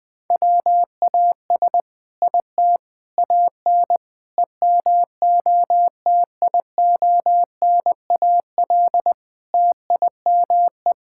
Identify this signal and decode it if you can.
Morse code, heard as a keyed tone, about 20 wpm, 700 Hz. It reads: WAS IT AN EMOTIONAL TIME
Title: Sentences from Top 1000 Words - Review Part 5 - 20wpm